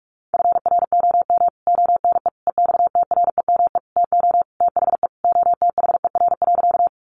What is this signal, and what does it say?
Morse code, heard as a keyed tone, about 44 wpm, 700 Hz. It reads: FROM ONE EXTREME TO THE OTHER.
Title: Sentences from Top 1300 Words 45wpm